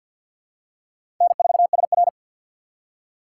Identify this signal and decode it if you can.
Morse code, heard as a keyed tone, about 53 wpm, 700 Hz. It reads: N4SR
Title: K1USN SST QSO Practice - 50wpm